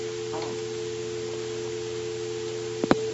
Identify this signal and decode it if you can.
Dial tone (North America), 350 + 440 Hz.